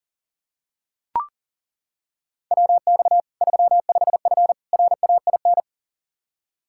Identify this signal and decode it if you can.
Morse code, heard as a keyed tone, about 40 wpm, 700 Hz. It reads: WX 35F RAIN
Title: K1USN SST QSO Practice 40wpm